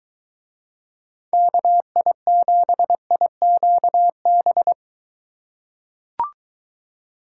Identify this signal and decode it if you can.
Morse code, heard as a keyed tone, about 21 wpm, 700 Hz. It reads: KI7IQB